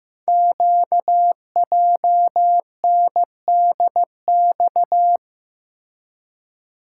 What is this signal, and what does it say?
Morse code, heard as a keyed tone, about 15 wpm, 700 Hz. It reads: QJNDX
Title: Code Groups - 5 Letters - 15wpm